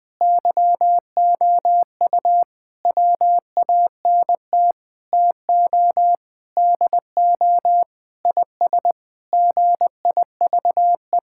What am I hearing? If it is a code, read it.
Morse code, heard as a keyed tone, about 20 wpm, 700 Hz. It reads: YOU WANT TO DO IS GIVE
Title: Sentences with 7-Word Phrases - Spoken Beginning - 20wpm